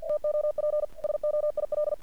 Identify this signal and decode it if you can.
Morse code, heard as a keyed tone, about 48 wpm, 600 Hz. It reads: TOP SOIL
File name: archived / CRSnet